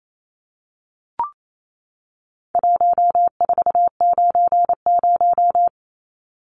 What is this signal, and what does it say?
Morse code, heard as a keyed tone, about 28 wpm, 700 Hz. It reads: 1490
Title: Numbers - 4-Digit 28wpm